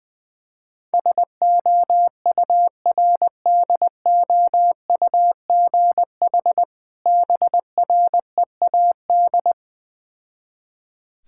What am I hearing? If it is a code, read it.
Morse code, heard as a keyed tone, about 20 wpm, 690 Hz. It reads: SOURDOUGH BREAD
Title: Difficult Words with Hints - 20wpm